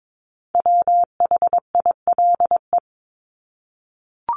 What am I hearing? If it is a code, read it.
Morse code, heard as a keyed tone, about 22 wpm, 700 Hz. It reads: WHILE
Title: Top 300 Words Repeated 22wpm